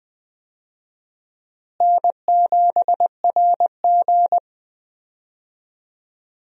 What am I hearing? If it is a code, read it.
Morse code, heard as a keyed tone, about 20 wpm, 700 Hz. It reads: N7RG
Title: USA Call Signs - 4 Characters - Repeated 2x - 20wpm